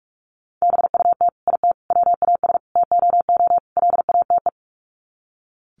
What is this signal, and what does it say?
Morse code, heard as a keyed tone, about 44 wpm, 700 Hz. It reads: BUT IT WAS TOO LATE
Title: Sets of 5 Words Encore2 45wpm